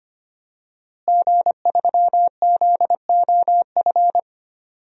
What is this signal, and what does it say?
Morse code, heard as a keyed tone, about 25 wpm, 700 Hz. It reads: G3ZOF
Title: DXCC-100 Call Signs - 5 Characters - Repeated 2x - 25wpm